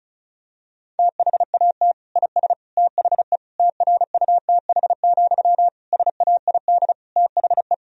Morse code, heard as a keyed tone, about 34 wpm, 700 Hz. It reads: THAT IS THE TRUTH, SAID THE
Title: Sentences from Top 1000 Words - Review Part 1 - 35wpm